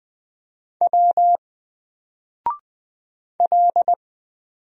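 Morse code, heard as a keyed tone, about 20 wpm, 700 Hz. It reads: W L
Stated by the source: Single letter and number Morse drill at 20 wpm, from N4LCD.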